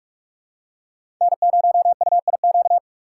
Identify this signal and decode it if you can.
Morse code, heard as a keyed tone, about 47 wpm, 700 Hz. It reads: N0UIQ
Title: USA Call Signs - 5 Characters - 4x Character Spacing - 45wpm